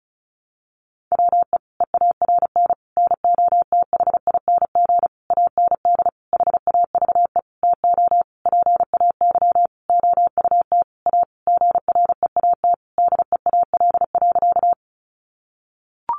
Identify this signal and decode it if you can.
Morse code, heard as a keyed tone, about 34 wpm, 700 Hz. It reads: WE EARN NOTHING AND HAVE TO PAY OUT A GREAT DEAL.
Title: Sentences from Top 1500 Words 35wpm